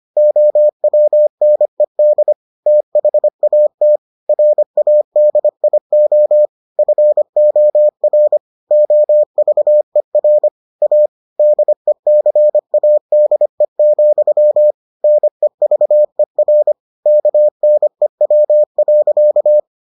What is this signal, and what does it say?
Morse code, heard as a keyed tone, about 25 wpm, 600 Hz. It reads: OWNED THAT RADIO FOR OVER A DECADE, NEVER KNEW.